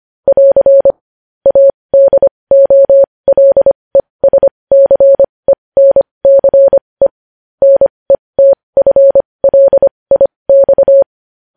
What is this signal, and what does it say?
Morse code, heard as a keyed tone, about 25 wpm, spaced as for 20 wpm, 550 Hz. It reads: <AR> ADOLESCENCE NETFLIX
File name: podcast / News headlines-1x-25x20